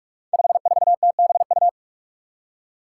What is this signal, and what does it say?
Morse code, heard as a keyed tone, about 44 wpm, 700 Hz. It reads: 54TBU
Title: Five Challenging Letter-Number Combinations 45wpm